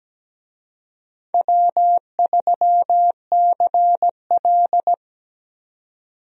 Morse code, heard as a keyed tone, about 17 wpm, 700 Hz. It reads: W3CL
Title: USA Call Signs - 4 Characters - 17wpm